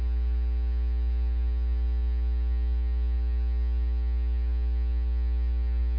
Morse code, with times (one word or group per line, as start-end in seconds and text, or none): none